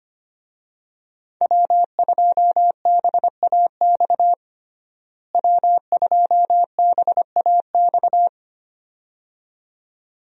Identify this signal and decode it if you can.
Morse code, heard as a keyed tone, about 25 wpm, 700 Hz. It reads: W2BAX W2BAX
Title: USA Call Signs - 5 Characters - Repeated 2x - 25wpm